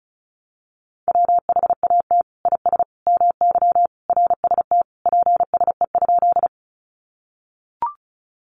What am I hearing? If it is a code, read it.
Morse code, heard as a keyed tone, about 34 wpm, 700 Hz. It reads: WHAT IS MY RST PSE?
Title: Common QSO Phrases 35wpm